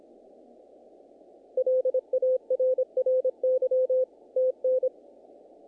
Morse code, heard as a keyed tone, about 25 wpm, 510 Hz. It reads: LARRY TN
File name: contests / naqp-cw-k5tr-op-k5pi- -1800